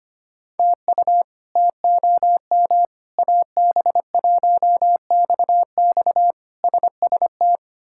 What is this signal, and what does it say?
Morse code, heard as a keyed tone, about 25 wpm, 700 Hz. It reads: TU TOM AB1XX SST
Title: K1USN SST QSO Practice - 25wpm